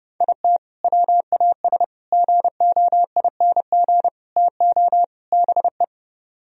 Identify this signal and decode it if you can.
Morse code, heard as a keyed tone, about 29 wpm, 700 Hz. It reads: IT WAS GOING TO BE